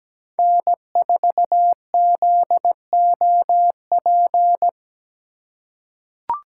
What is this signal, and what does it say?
Morse code, heard as a keyed tone, about 17 wpm, 700 Hz. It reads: N4ZOP
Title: USA Call Signs - 5 Characters - 3x Character Spacing - 17wpm